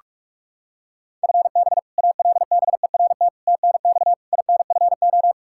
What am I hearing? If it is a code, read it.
Morse code, heard as a keyed tone, about 46 wpm, 700 Hz. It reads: FB ALBERT TNX INFO
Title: Practice QSO - Exchange 3 - More Info - 45wpm